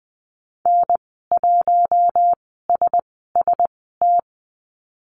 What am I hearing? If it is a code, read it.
Morse code, heard as a keyed tone, about 20 wpm, spaced as for 14 wpm, 700 Hz. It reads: N1SST
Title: 042 Lesson 8g - Number 1 - Call Signs 20wpm